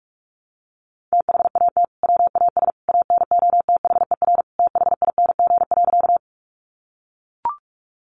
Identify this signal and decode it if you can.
Morse code, heard as a keyed tone, about 44 wpm, 700 Hz. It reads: THAT WAS ANOTHER THING.
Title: Sentences from Top 600 Words 45wpm